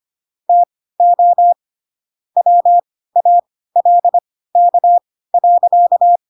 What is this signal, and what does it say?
Morse code, heard as a keyed tone, about 25 wpm, spaced as for 16 wpm, 700 Hz. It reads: TO WALK.
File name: Pinocchio / Pinocchio ch3 10 25 ik1hge